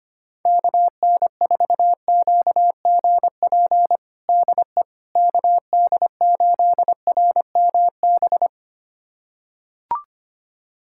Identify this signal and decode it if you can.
Morse code, heard as a keyed tone, about 25 wpm, 700 Hz. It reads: KN4QGP DE KD8RMB
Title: USA Callsign DE Callsign 25wpm